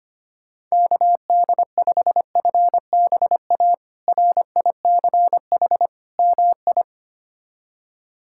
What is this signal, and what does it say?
Morse code, heard as a keyed tone, about 25 wpm, 700 Hz. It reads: KD5FBA RICH MI
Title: K1USN SST QSO Practice - 25wpm